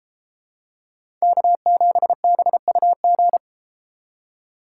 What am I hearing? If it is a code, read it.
Morse code, heard as a keyed tone, about 32 wpm, 700 Hz. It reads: K7BUG